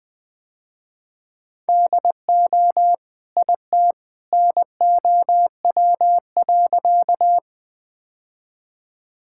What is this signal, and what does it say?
Morse code, heard as a keyed tone, about 20 wpm, 700 Hz. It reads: DO IT NOW.
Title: Sentences with 2-Word Phrases - Spoken Beginning - 20wpm